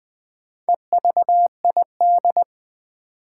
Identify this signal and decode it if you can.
Morse code, heard as a keyed tone, about 21 wpm, 700 Hz. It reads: EVID
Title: Build A Word - 4 Letters At A Time - 8-Letter Words - 20wpm